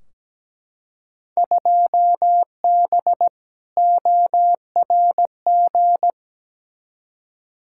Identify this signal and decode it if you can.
Morse code, heard as a keyed tone, about 17 wpm, 700 Hz. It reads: 2B ORG
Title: Field Day Exchanges Encore - 17wpm